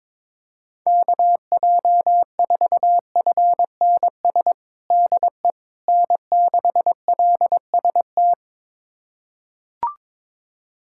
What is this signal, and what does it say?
Morse code, heard as a keyed tone, about 22 wpm, 700 Hz. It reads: KJ4FNS DE N6LST
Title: USA Callsign DE Callsign 22wpm